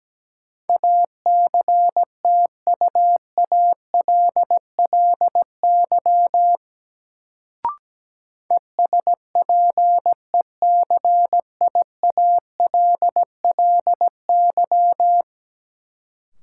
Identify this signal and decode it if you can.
Morse code, heard as a keyed tone, about 17 wpm, 700 Hz. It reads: ACTUALLY ESPECIALLY
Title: Sentences from Top 700 Words - New Word Drill - 17wpm